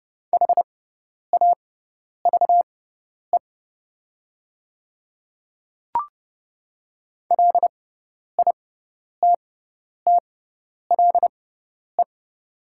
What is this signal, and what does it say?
Morse code, heard as a keyed tone, about 30 wpm, spaced as for 10 wpm, 700 Hz. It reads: HAVE LITTLE
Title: 6x Character Spacing - Top 200 Words - 30wpm